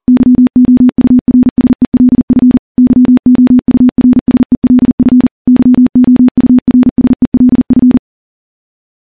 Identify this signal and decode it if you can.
Morse code, heard as a keyed tone, about 40 wpm, 250 Hz. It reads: YOURSELF YOURSELF YOURSELF
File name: morse / morse-training-40WPM